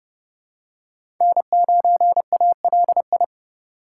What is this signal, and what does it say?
Morse code, heard as a keyed tone, about 29 wpm, 700 Hz. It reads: N9ALI
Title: USA Call Signs - 5 Characters - 3x Character Spacing - 30wpm